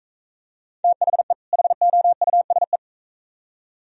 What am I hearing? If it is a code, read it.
Morse code, heard as a keyed tone, about 44 wpm, 690 Hz. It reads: THE HOUSE